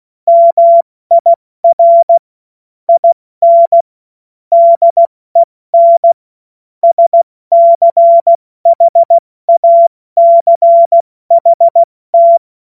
Morse code, heard as a keyed tone, about 16 wpm, 680 Hz. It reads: MIR IN DEN SCHACHT
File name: Morsekurs / Witze / Witz 03. August 2021, 70 F80